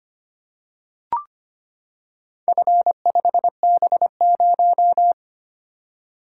Morse code, heard as a keyed tone, about 25 wpm, 700 Hz. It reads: F5B0
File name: practice / 25wpm / Four Letter-Number Combinations - 25wpm